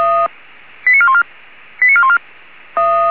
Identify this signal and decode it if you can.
Five-tone selective call (ZVEI), heard as quick runs of five tones, tones of about 70 ms. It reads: 89414 89414